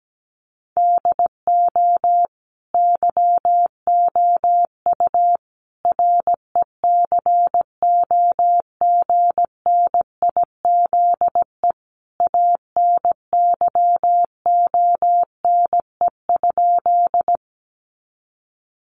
Morse code, heard as a keyed tone, about 17 wpm, 700 Hz. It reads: DO YOU RECOGNIZE ANYONE?
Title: Sentences from Top 1400 Words 17wpm